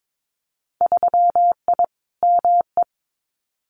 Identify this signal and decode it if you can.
Morse code, heard as a keyed tone, about 22 wpm, 700 Hz. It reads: 3I ME
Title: Winter Field Day Exchanges 22wpm